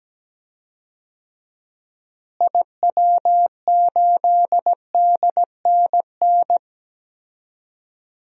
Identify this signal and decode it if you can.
Morse code, heard as a keyed tone, about 17 wpm, 700 Hz. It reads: IW8DNN
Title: DXCC-100 Call Signs - 6 Characters - Repeated 3x - 17wpm